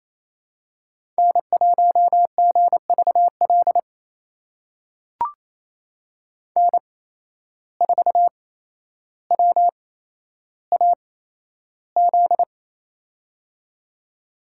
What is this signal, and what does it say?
Morse code, heard as a keyed tone, about 28 wpm, spaced as for 12 wpm, 700 Hz. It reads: N1GVL N4WAZ
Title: USA Call Signs - 5 Characters - 8x Character Spacing - 28wpm